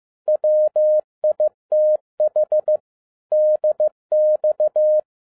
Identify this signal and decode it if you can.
Morse code, heard as a keyed tone, about 15 wpm, 610 Hz. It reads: WITH DX
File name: 2011-07-mp3 / n22 QRP OTAKU 15wpm